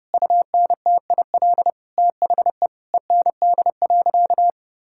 Morse code, heard as a keyed tone, about 30 wpm, 700 Hz. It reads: UNTIL THE END.
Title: Sentences from Top 1000 Words - Review Part 5 - 30wpm